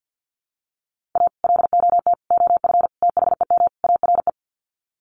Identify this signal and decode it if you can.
Morse code, heard as a keyed tone, about 50 wpm, 700 Hz. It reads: A LOT OF THEM ARE
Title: Sets of 5 Words Encore2 50wpm